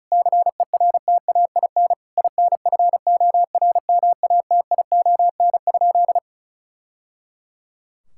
Morse code, heard as a keyed tone, about 36 wpm, 700 Hz. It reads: CERTAIN INFORMATION?
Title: Sentences from Top 1000 Words - Review Part 6 - 35wpm